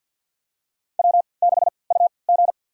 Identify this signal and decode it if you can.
Morse code, heard as a keyed tone, about 48 wpm, spaced as for 28 wpm, 700 Hz. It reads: W6UG
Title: USA Call Signs - 4 Characters - 3x Character Spacing - 50wpm